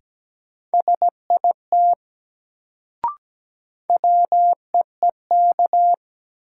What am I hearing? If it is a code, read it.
Morse code, heard as a keyed tone, about 17 wpm, 700 Hz. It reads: SIT WEEK
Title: Sentences from Top 500 Words - New Word Drill - 17wpm